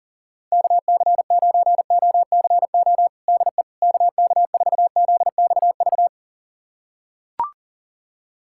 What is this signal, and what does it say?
Morse code, heard as a keyed tone, about 40 wpm, 700 Hz. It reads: KC9OCO DE KK4ZXV
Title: USA Callsign DE Callsign 40wpm